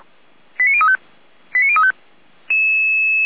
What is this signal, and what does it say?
Five-tone selective call (ZVEI), heard as quick runs of five tones, tones of about 70 ms. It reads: 89035 89035
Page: /455/SDR/ZVEI/89042/